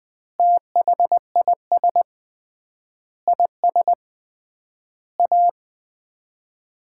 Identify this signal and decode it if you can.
Morse code, heard as a keyed tone, about 19 wpm, 700 Hz. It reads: THIS IS A
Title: 3x Word-Spacing - Sets of 5 Words - 20wpm